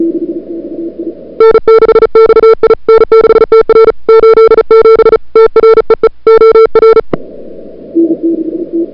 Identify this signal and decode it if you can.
Morse code, heard as a keyed tone, about 37 wpm, 440 Hz. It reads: N6XI N6TR 87 TREE OR
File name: n6tr / Fab2007CWSprintHour1